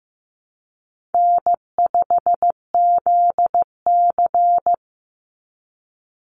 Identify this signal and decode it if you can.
Morse code, heard as a keyed tone, about 15 wpm, 700 Hz. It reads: N5ZC
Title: NA SOTA Chaser Callsigns 15wpm